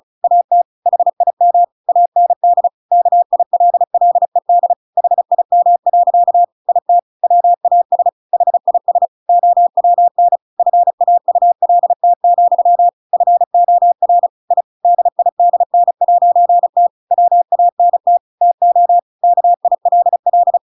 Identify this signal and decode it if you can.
Morse code, heard as a keyed tone, about 34 wpm, 700 Hz. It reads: AT HIM AND KILLED HIM. IT WAS HIS OWN FAULT, FOR I DIDN'T WANT TO KILL
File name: Pinocchio / Pinocchio ch7 35 35 ik1hge